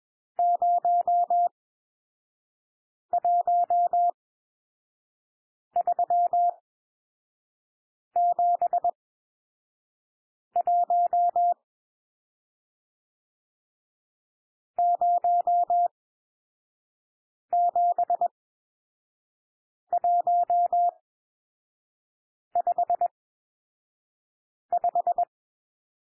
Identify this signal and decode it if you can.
Morse code, heard as a keyed tone, about 21 wpm, spaced as for 5 wpm, 700 Hz. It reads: 01371 07155
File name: cwmp3 / N1 SOUND NUMBERS 21 wpm